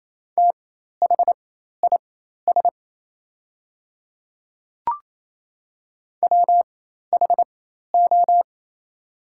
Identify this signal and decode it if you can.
Morse code, heard as a keyed tone, about 27 wpm, spaced as for 13 wpm, 700 Hz. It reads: THIS WHO
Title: 4x Character Spacing - Top 300 Words - 28wpm